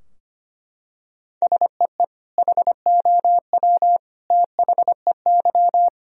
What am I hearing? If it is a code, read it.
Morse code, heard as a keyed tone, about 25 wpm, 700 Hz. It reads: SEE HOW THEY
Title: Sentences from Top 1000 Words - Review Part 2 - 25wpm